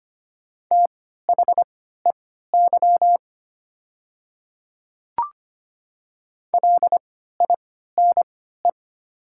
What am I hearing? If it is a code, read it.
Morse code, heard as a keyed tone, about 25 wpm, spaced as for 14 wpm, 690 Hz. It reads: THEY LINE
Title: 3x Character Spacing - Top 200 Words - 25wpm